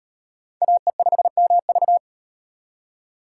Morse code, heard as a keyed tone, about 36 wpm, 700 Hz. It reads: AE5MV